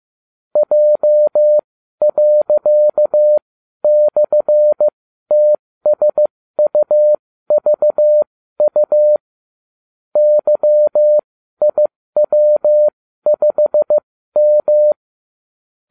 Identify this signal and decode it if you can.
Morse code, heard as a keyed tone, about 15 wpm, spaced as for 12 wpm, 600 Hz. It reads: J./TSUVU YIW5M